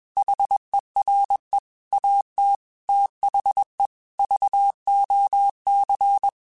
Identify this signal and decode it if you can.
Morse code, heard as a keyed tone, about 21 wpm, 800 Hz. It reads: HERE AT THE VOC
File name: QSOs / QSOs-mp3s / WD0BGP 1@20